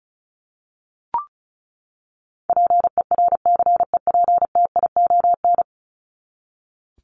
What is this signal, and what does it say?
Morse code, heard as a keyed tone, about 34 wpm, 700 Hz. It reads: PERCEPTION
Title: Top 1751-2000 Words 35wpm